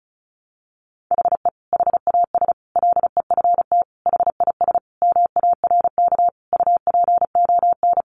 Morse code, heard as a keyed tone, about 34 wpm, 700 Hz. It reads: HE HAS LEFT HIS MARK UPON
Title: Sentences from Top 1100 Words 35wpm